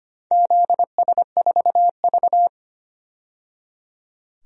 Morse code, heard as a keyed tone, about 25 wpm, 700 Hz. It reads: ZS4V
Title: DXCC-100 Call Signs - 4 Characters - 25wpm